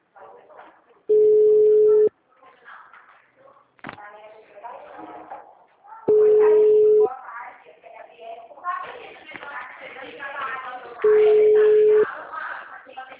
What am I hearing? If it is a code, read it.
Ringback tone (Europe, 425 Hz), with speech.